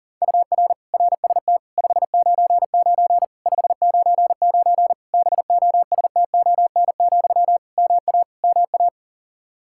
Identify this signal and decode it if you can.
Morse code, heard as a keyed tone, about 37 wpm, 700 Hz. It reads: UR RST 599 599 BOSTON, MA MA
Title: Practice QSO - Exchange 1 - Beginning - 40wpm